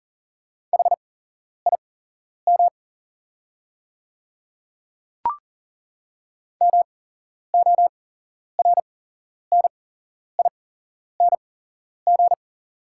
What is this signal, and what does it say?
Morse code, heard as a keyed tone, about 40 wpm, spaced as for 11 wpm, 690 Hz. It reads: HIM MORNING
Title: 8x Character Spacing - Top 500 Words - 40wpm